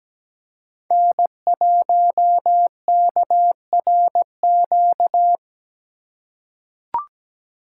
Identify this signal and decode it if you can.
Morse code, heard as a keyed tone, about 17 wpm, 700 Hz. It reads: N1KRQ
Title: USA Call Signs - 2x Character Spacing - 17wpm